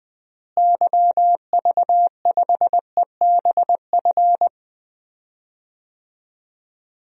Morse code, heard as a keyed tone, about 20 wpm, 700 Hz. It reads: YV5EBF
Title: DXCC-100 Call Signs - 6 Characters - Repeated 2x - 20wpm